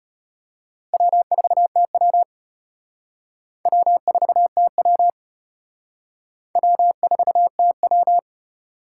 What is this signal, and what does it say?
Morse code, heard as a keyed tone, about 32 wpm, 700 Hz. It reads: W4TW W4TW W4TW